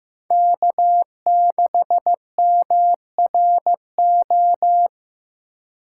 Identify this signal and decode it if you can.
Morse code, heard as a keyed tone, about 15 wpm, 700 Hz. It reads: K6MRO
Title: USA Call Signs - 5 Characters - 4x Character Spacing - 15wpm